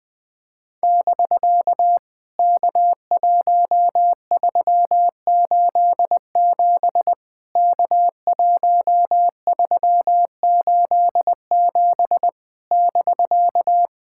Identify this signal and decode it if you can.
Morse code, heard as a keyed tone, about 19 wpm, 700 Hz. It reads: <BK> K1387 K1387 <BK>
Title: POTA QSO Practice - 20wpm